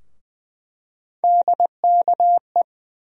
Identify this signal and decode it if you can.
Morse code, heard as a keyed tone, about 20 wpm, 700 Hz. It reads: DKE
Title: USA Call Sign Postfixes - 3 Letters 20wpm